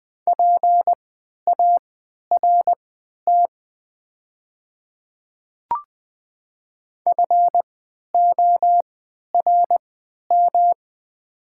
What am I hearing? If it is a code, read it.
Morse code, heard as a keyed tone, about 20 wpm, spaced as for 11 wpm, 700 Hz. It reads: PART FORM
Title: 3x Character Spacing - Top 200 Words - 20wpm